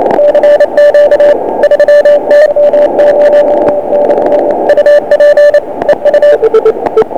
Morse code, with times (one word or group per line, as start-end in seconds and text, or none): none